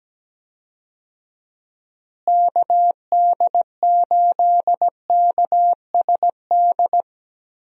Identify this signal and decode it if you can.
Morse code, heard as a keyed tone, about 17 wpm, 700 Hz. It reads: KD8KSD
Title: USA Call Signs - 6 Characters - Repeated 2x - 17wpm